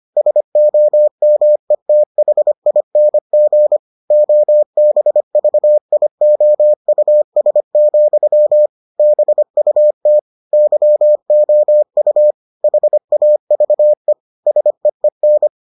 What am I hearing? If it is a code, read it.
Morse code, heard as a keyed tone, about 25 wpm, 600 Hz. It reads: SOMETHING OBVIOUS, BUT YOU HAVE SEEN